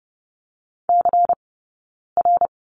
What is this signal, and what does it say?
Morse code, heard as a keyed tone, about 30 wpm, 700 Hz. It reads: C R
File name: 092 Lesson 16d - Letter c - Two Characters 30wpm